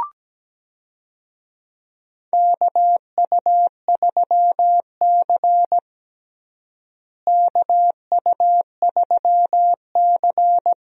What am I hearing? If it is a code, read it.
Morse code, heard as a keyed tone, about 17 wpm, 700 Hz. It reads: KU3C KU3C
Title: USA Call Signs - 4 Characters - Repeated 3x - 17wpm